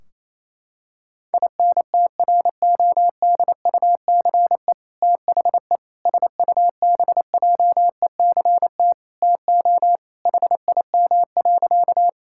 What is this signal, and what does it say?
Morse code, heard as a keyed tone, about 28 wpm, 700 Hz. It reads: INTRODUCE THE SUBJECT TO HIM.
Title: Sentences from Top 1600 Words 28wpm